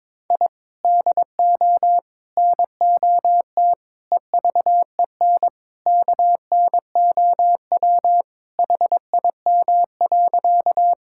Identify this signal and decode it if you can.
Morse code, heard as a keyed tone, about 22 wpm, 700 Hz. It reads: I DO NOT EVEN KNOW HIM.
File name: practice / 22wpm / Sentences with 4-Word Phrases - 22wpm